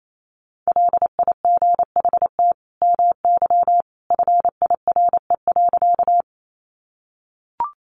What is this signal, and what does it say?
Morse code, heard as a keyed tone, about 28 wpm, 700 Hz. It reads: LIGHT MY FIRE.
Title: Sentences from Top 400 Words 28wpm